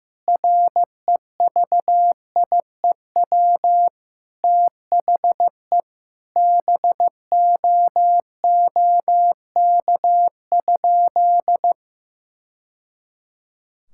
Morse code, heard as a keyed tone, about 15 wpm, 700 Hz. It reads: REVIEW THE BOOK?
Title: Sentences from Top 1000 Words - Review Part 6 - 15wpm